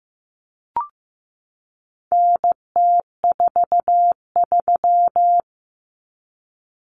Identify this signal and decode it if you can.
Morse code, heard as a keyed tone, about 15 wpm, 700 Hz. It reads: NT43
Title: Common Cut Numbers - 4-Digit 15wpm